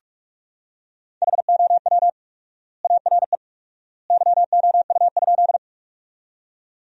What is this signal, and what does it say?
Morse code, heard as a keyed tone, about 41 wpm, 700 Hz. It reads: HOW ARE YOU?
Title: 4x Word-Spacing - Sentences from Top 100 Words - 45wpm